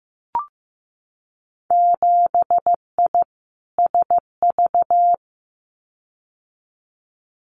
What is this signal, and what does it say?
Morse code, heard as a keyed tone, about 15 wpm, 700 Hz. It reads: 7I SV
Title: Winter Field Day Exchanges Encore 15wpm